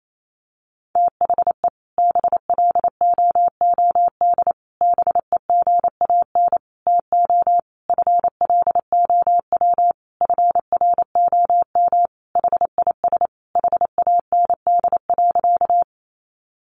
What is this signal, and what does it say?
Morse code, heard as a keyed tone, about 28 wpm, 700 Hz. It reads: THE BLOOD BEGAN TO FLOW FROM HIS HAND.
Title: Sentences from Top 1200 Words 28wpm